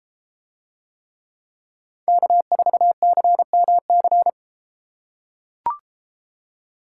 Morse code, heard as a keyed tone, about 33 wpm, 700 Hz. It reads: K4CMC